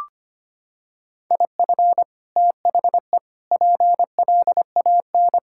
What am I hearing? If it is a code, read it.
Morse code, heard as a keyed tone, about 25 wpm, 700 Hz. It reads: IF THE PLAN
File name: practice / 25wpm / Sentences with 2-Word Phrases - 25wpm